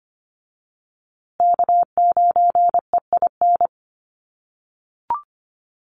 Morse code, heard as a keyed tone, about 25 wpm, 700 Hz. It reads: K9EIN
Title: USA Call Signs Repeated 2x 25wpm